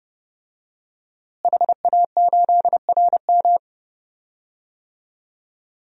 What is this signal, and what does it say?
Morse code, heard as a keyed tone, about 30 wpm, 700 Hz. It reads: HA8RM
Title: DXCC-100 Call Signs - 5 Characters - Repeated 2x - 30wpm